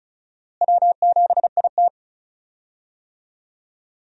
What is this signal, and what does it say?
Morse code, heard as a keyed tone, about 36 wpm, 700 Hz. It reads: W7IT
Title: USA Call Signs - 4 Characters - Repeated 3x - 35wpm